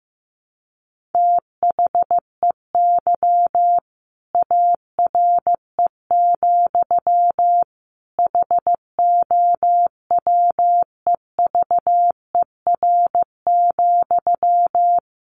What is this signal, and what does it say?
Morse code, heard as a keyed tone, about 15 wpm, 700 Hz. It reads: THEY ARE, HOWEVER,
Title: Sentences from Top 1200 Words 15wpm